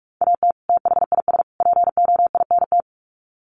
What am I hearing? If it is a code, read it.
Morse code, heard as a keyed tone, about 44 wpm, 700 Hz. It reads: AT THIS POINT
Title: Sets of 3 Words 45wpm